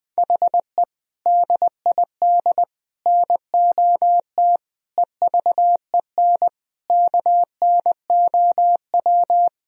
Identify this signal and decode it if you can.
Morse code, heard as a keyed tone, about 20 wpm, 700 Hz. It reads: HE DID NOT EVEN KNOW
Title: Sentences from Top 1000 Words - Review Part 1 - 20wpm